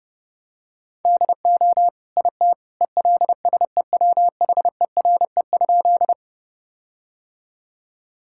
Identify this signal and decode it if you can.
Morse code, heard as a keyed tone, about 30 wpm, 700 Hz. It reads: DO IT ELSEWHERE?
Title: Sentences from Top 1600 Words 30wpm